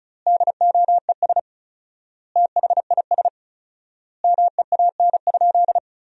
Morse code, heard as a keyed tone, about 34 wpm, 700 Hz. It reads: DOES THIS MEAN?
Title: 4x Word-Spacing - Sentences from Top 200 Words - 35wpm